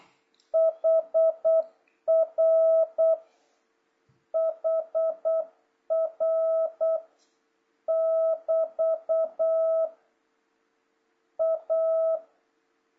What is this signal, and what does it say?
Morse code, heard as a keyed tone, about 8 wpm, 650 Hz. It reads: HR HR <BT> A
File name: jh8cbh / 2026cw